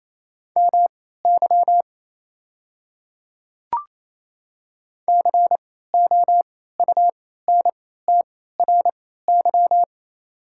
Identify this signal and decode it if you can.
Morse code, heard as a keyed tone, about 27 wpm, spaced as for 16 wpm, 700 Hz. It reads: MY COUNTRY
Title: 3x Character Spacing - Top 500 Words - 28wpm